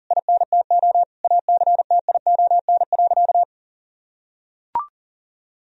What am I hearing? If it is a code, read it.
Morse code, heard as a keyed tone, about 40 wpm, 700 Hz. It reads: INTO ACTION.
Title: Sentences from Top 1000 Words - Review Part 4 - 40wpm